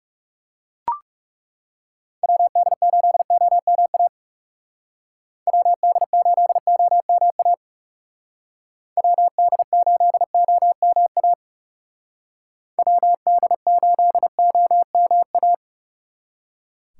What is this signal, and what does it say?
Morse code, heard as a keyed tone, about 35 wpm, 700 Hz. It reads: WD8OMA WD8OMA WD8OMA WD8OMA
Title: Speed Racing - USA Call Signs 30wpm